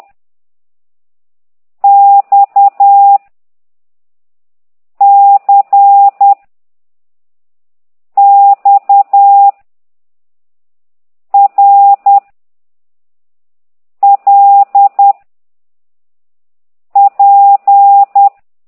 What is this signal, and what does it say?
Morse code, heard as a keyed tone, about 10 wpm, 800 Hz. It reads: X C X R L P
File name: ct2idw / exerc9